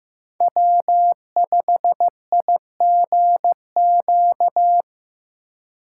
Morse code, heard as a keyed tone, about 15 wpm, 700 Hz. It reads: W5IGQ
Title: USA Call Signs - 5 Characters - 2x Character Spacing - 15wpm